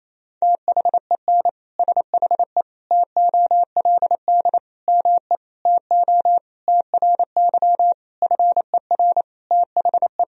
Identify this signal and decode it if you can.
Morse code, heard as a keyed tone, about 27 wpm, 700 Hz. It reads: THEN SHE TOLD ME TO TRY FER THE
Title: Sentences from Top 400 Words and Abbreviations 28wpm